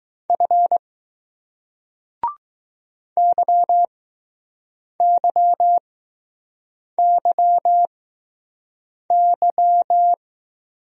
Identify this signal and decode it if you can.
Morse code, heard as a keyed tone, about 20 wpm, spaced as for 6 wpm, 700 Hz. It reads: F YYYY